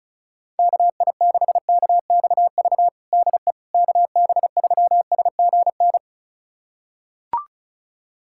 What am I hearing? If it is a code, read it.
Morse code, heard as a keyed tone, about 36 wpm, 700 Hz. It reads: KI6KXV DE KB3SGN
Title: USA Callsign DE Callsign 35wpm